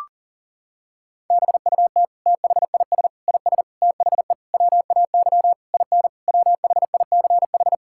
Morse code, heard as a keyed tone, about 40 wpm, 700 Hz. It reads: BUT THIS IS THE WAY IN WHICH
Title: Sentences from Top 1000 Words - Review Part 5 - 40wpm